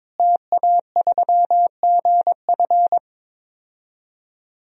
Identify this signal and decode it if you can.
Morse code, heard as a keyed tone, about 22 wpm, 700 Hz. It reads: TA3GF
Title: DXCC-100 Call Signs - 5 Characters - Repeated 3x - 22wpm